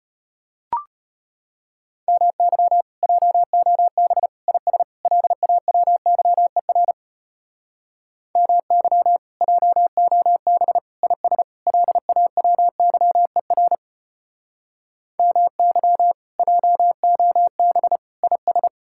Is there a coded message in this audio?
Yes: Morse code from 2.1 to 18.7 s, heard as a keyed tone, about 32 wpm, 700 Hz. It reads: MY JOB IS LAWYER MY JOB IS LAWYER MY JOB IS